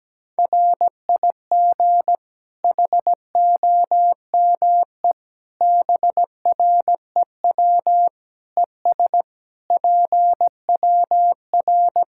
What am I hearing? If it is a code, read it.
Morse code, heard as a keyed tone, about 17 wpm, 700 Hz. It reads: RIG HOME BREW ES PWR
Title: Practice QSO - Exchange 2 - Info - 17wpm